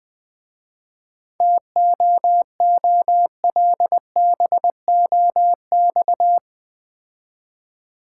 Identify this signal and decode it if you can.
Morse code, heard as a keyed tone, about 22 wpm, 700 Hz. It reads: TOOLBOX
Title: Difficult Words - Speed-Racing with Hints - 20wpm